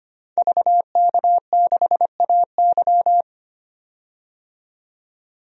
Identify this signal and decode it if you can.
Morse code, heard as a keyed tone, about 25 wpm, 700 Hz. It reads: VK6AY
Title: DXCC-100 Call Signs - 5 Characters - 25wpm